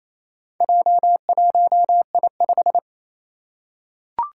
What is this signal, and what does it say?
Morse code, heard as a keyed tone, about 29 wpm, 700 Hz. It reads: J1I5
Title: Four Challenging Letter-Number Combinations 28wpm